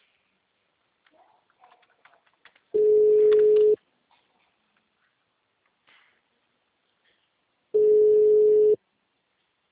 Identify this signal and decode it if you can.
Ringback tone (Europe, 425 Hz).